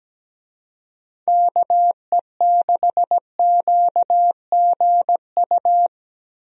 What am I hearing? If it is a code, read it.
Morse code, heard as a keyed tone, about 17 wpm, 700 Hz. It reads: KE6QGU
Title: USA Call Signs - 6 Characters - Repeated 3x - 17wpm